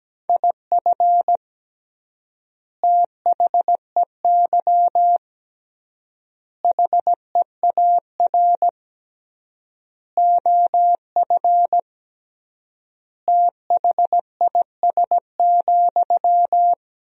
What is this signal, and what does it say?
Morse code, heard as a keyed tone, about 17 wpm, 700 Hz. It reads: IF THEY HEAR OF THIS,
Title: 3x Word-Spacing - Sentences from Top 400 Words - 17wpm